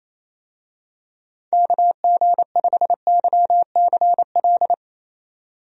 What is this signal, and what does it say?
Morse code, heard as a keyed tone, about 28 wpm, 700 Hz. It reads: KG5YCL
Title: USA Call Signs - 6 Characters - Repeated 3x - 28wpm